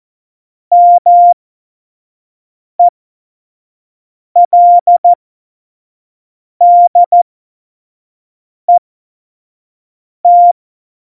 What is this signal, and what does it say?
Morse code, heard as a keyed tone, about 14 wpm, spaced as for 5 wpm, 700 Hz. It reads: MELDET
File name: Morsekurs / Witze / Witz 09. April 2019, 25 F70